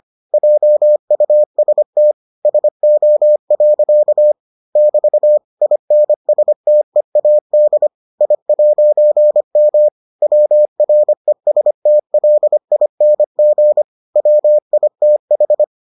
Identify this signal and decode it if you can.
Morse code, heard as a keyed tone, about 25 wpm, 600 Hz. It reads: JUST SO. <BT> INSTEAD I'M WRESTLING WITH